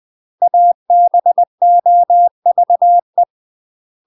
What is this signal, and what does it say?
Morse code, heard as a keyed tone, about 20 wpm, 700 Hz. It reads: ABOVE
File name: UK MASTER QSOs TRAINING / 20WPM / UK MASTER QSOs 0030